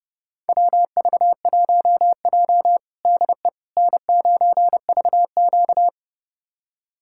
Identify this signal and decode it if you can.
Morse code, heard as a keyed tone, about 30 wpm, 700 Hz. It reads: WV1J DE N9VQ
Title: USA Callsign DE Callsign - 4 Characters - 30wpm